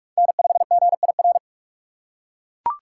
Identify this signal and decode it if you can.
Morse code, heard as a keyed tone, about 40 wpm, 700 Hz. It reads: N5GIR